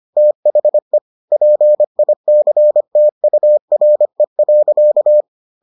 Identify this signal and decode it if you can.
Morse code, heard as a keyed tone, about 25 wpm, 600 Hz. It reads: THE PICTURE.